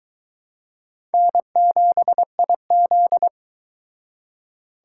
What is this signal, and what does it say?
Morse code, heard as a keyed tone, about 24 wpm, 700 Hz. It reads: N7IZ